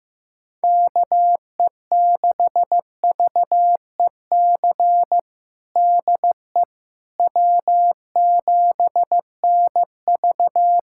Morse code, heard as a keyed tone, about 15 wpm, 700 Hz. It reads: KE6VEC DE W7NV
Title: USA Callsign DE Callsign 15wpm